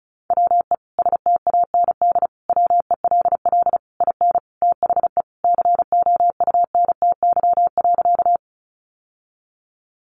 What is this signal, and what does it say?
Morse code, heard as a keyed tone, about 34 wpm, 700 Hz. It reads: WE STAND WELL IN THE COUNTY.
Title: Sentences from Top 1200 Words 35wpm